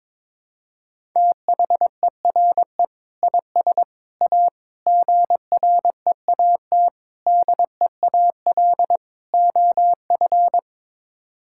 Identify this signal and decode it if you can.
Morse code, heard as a keyed tone, about 22 wpm, 700 Hz. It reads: THERE IS A GREAT DEAL OF
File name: practice / 22wpm / Sets of 6 Words - 22wpm